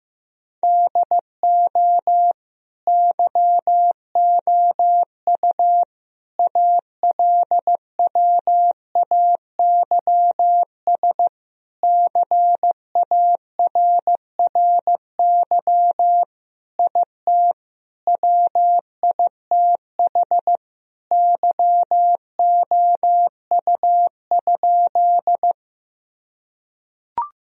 Morse code, heard as a keyed tone, about 16 wpm, 700 Hz. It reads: DO YOU ALWAYS CARRY IT WITH YOU?
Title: Sentences from Top 1000 Words - Review Part 1 - 15wpm